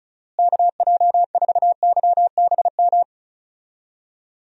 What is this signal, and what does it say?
Morse code, heard as a keyed tone, about 36 wpm, 700 Hz. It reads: KJ4YBM
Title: USA Call Signs - 6 Characters - Repeated 2x - 35wpm